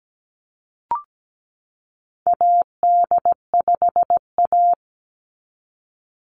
Morse code, heard as a keyed tone, about 17 wpm, 700 Hz. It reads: AD5A
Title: NA SOTA Activator Callsigns 17wpm